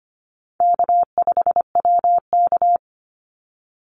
Morse code, heard as a keyed tone, about 25 wpm, 700 Hz. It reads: K5WK
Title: Easy Call Signs - 25wpm